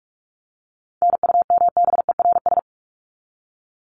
Morse code, heard as a keyed tone, about 44 wpm, 700 Hz. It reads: NUMBERS
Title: ES:S Suffix - Top 100 Words with Base Words 45wpm